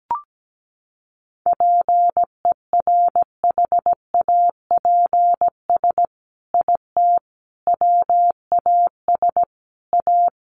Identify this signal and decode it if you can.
Morse code, heard as a keyed tone, about 17 wpm, 700 Hz. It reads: PERHAPS IT WAS A
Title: Sentences from Top 1400 Words 17wpm